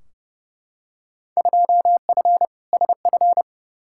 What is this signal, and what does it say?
Morse code, heard as a keyed tone, about 31 wpm, 700 Hz. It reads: 2F SF
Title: Field Day Exchanges - 30wpm